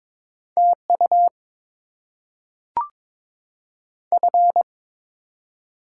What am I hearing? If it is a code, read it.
Morse code, heard as a keyed tone, about 22 wpm, 700 Hz. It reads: T U F
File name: Top 20 QSO Elements - 8x Character Spacing - 22wpm